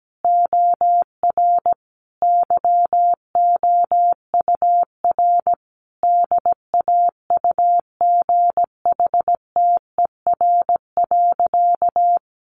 Morse code, heard as a keyed tone, about 17 wpm, 700 Hz. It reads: OR YOUR DAUGHTER.
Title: Sentences from Top 1500 Words 17wpm